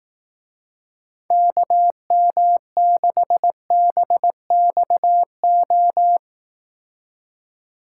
Morse code, heard as a keyed tone, about 18 wpm, 700 Hz. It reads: KM6BXO
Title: Speed Racing - USA 2x3 Call Signs 15wpm